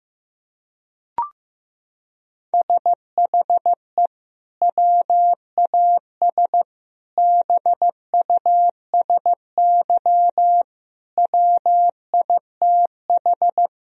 Morse code, heard as a keyed tone, about 16 wpm, 700 Hz. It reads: SHE WAS BUSY WITH
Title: Sentences from Top 1000 Words - Review Part 2 - 15wpm